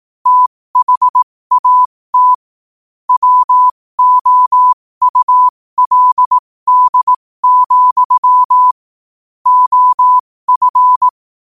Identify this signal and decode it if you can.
Morse code, heard as a keyed tone, about 18 wpm, spaced as for 15 wpm, 1000 Hz. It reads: THAT WOULD, OF
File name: qotdfiles / 15wpm